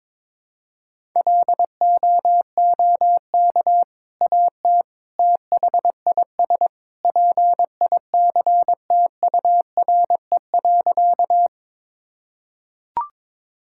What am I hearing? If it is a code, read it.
Morse code, heard as a keyed tone, about 22 wpm, 700 Hz. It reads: LOOK AT THIS PICTURE.
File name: practice / 22wpm / Sentences with 2-Word Phrases - 22wpm